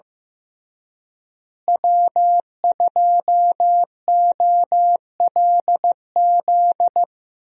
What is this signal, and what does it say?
Morse code, heard as a keyed tone, about 15 wpm, 700 Hz. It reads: W2OLZ
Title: USA Call Signs - 5 Characters - Repeated 3x - 15wpm